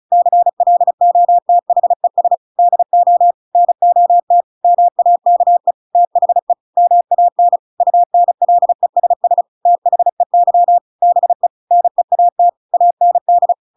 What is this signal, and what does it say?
Morse code, heard as a keyed tone, about 34 wpm, 700 Hz. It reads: CLOTHES DO NOT MAKE THE MAN UNLESS THEY BE NEAT AND
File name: Pinocchio / Pinocchio ch8 35 35 ik1hge